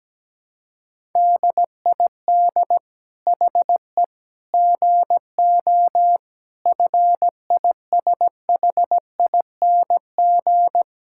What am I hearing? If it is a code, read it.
Morse code, heard as a keyed tone, about 17 wpm, 700 Hz. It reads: DID HE GO FISHING
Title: Sentences from Top 1000 Words - Review Part 5 - 17wpm